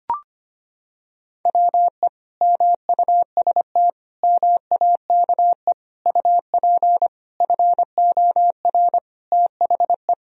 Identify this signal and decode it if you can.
Morse code, heard as a keyed tone, about 25 wpm, 700 Hz. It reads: WE MUST MAKE UP FOR THE
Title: Sentences from Top 1000 Words - Review Part 3 - 25wpm